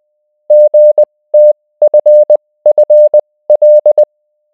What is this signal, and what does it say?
Morse code, heard as a keyed tone, about 20 wpm, spaced as for 16 wpm, 600 Hz. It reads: GTFFL